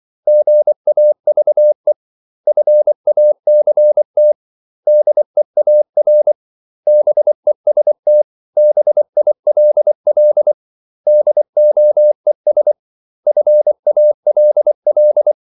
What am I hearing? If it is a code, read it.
Morse code, heard as a keyed tone, about 24 wpm, 600 Hz. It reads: GAVE FACT DEAR BEST BILL DOES FALL
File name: CW / Practice / 500MostCommonEnglishWords24